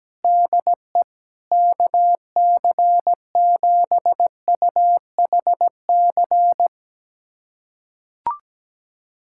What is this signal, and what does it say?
Morse code, heard as a keyed tone, about 17 wpm, 700 Hz. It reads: DE KC7UHC
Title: USA Callsign DE Callsign - 6 Characters - 17wpm